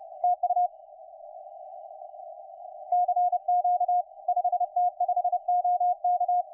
Morse code, heard as a keyed tone, about 30 wpm, 720 Hz. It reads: TU CQ 5T5OK
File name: 5T5OK 10CW